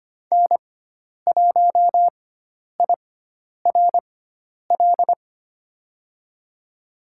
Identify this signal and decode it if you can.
Morse code, heard as a keyed tone, about 25 wpm, spaced as for 10 wpm, 700 Hz. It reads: N1IRL
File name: Lesson 16g - Letter c - Call Signs 25wpm